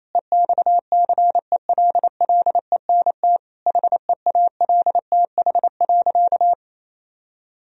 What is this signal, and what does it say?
Morse code, heard as a keyed tone, about 28 wpm, 700 Hz. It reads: EXCELLENT HEALTH.
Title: Sentences from Top 1000 Words - Review Part 3 - 28wpm